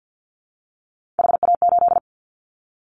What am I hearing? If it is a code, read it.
Morse code, heard as a keyed tone, about 50 wpm, 700 Hz. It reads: HA8
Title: Callsign Prefixes for Top 25 DXCC - 50wpm